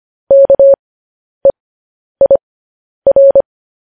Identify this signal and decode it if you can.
Morse code, heard as a keyed tone, about 25 wpm, spaced as for 10 wpm, 550 Hz. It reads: KEIR
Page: news headlines in morse code at 10 wpm